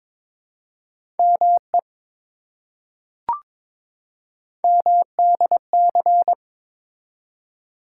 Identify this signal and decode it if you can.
Morse code, heard as a keyed tone, about 22 wpm, 700 Hz. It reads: ME MDC
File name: ARRL Sections v2 - 22wpm